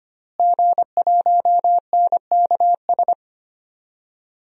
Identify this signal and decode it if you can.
Morse code, heard as a keyed tone, about 25 wpm, 700 Hz. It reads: G1NKS
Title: DXCC-100 Call Signs - 5 Characters - Repeated 3x - 25wpm